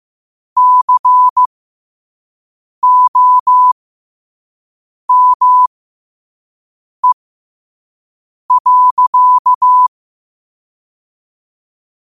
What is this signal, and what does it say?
Morse code, heard as a keyed tone, about 15 wpm, spaced as for 5 wpm, 1000 Hz. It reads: COME.
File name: qotdfiles / 5wpm